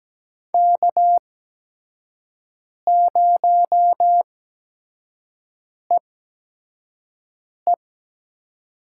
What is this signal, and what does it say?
Morse code, heard as a keyed tone, about 17 wpm, spaced as for 5 wpm, 700 Hz. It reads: K0EE
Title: USA Call Signs - 5 Characters - 8x Character Spacing - 17wpm